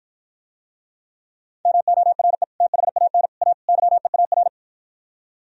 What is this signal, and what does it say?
Morse code, heard as a keyed tone, about 52 wpm, 700 Hz. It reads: MORE THAN A YEAR